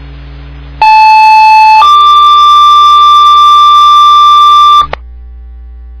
Two-tone page, heard as a one-second tone, then a three-second tone, about 870 then 1180 Hz, with noise. It is heard from 0.8 to 4.8 s.